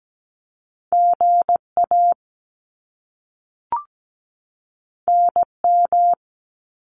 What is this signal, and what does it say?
Morse code, heard as a keyed tone, about 17 wpm, 700 Hz. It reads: GA NM
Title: US States 17wpm